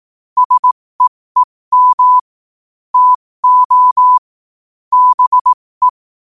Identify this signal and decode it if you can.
Morse code, heard as a keyed tone, about 18 wpm, spaced as for 15 wpm, 1000 Hz. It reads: SEEM TO BE
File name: qotdfiles / 15wpm